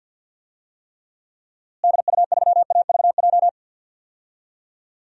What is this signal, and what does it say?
Morse code, heard as a keyed tone, about 50 wpm, 700 Hz. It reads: DU3AVJ
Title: DXCC-100 Call Signs - 6 Characters - Repeated 2x - 50wpm